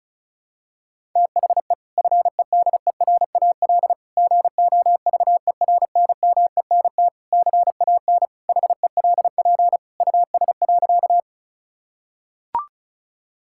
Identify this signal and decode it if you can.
Morse code, heard as a keyed tone, about 34 wpm, 700 Hz. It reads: THE FEDERAL GOVERNMENT CAN HELP US.
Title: Sentences from Top 1000 Words - Review Part 5 - 35wpm